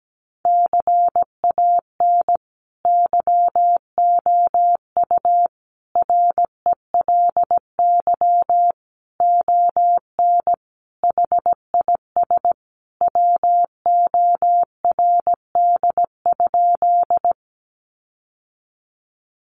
Morse code, heard as a keyed tone, about 17 wpm, 700 Hz. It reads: CAN YOU RELY ON HIS WORD?
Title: Sentences from Top 1200 Words 17wpm